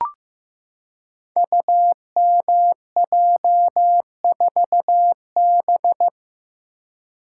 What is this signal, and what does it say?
Morse code, heard as a keyed tone, about 15 wpm, 700 Hz. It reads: UMJ4B
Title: Code Groups - 5 Letter:Numbers - 15wpm